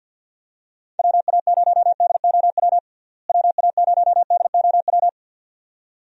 Morse code, heard as a keyed tone, about 51 wpm, 700 Hz. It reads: WA0DOW WA0DOW
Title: USA Call Signs - 6 Characters - Repeated 2x - 50wpm